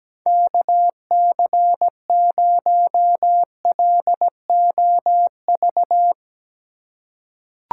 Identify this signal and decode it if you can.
Morse code, heard as a keyed tone, about 17 wpm, 700 Hz. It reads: KC0LOV
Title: USA Call Signs - 4x Character Spacing - 17wpm